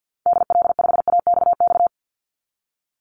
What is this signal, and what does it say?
Morse code, heard as a keyed tone, about 50 wpm, 700 Hz. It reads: DL5AXX
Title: Call Signs - 50wpm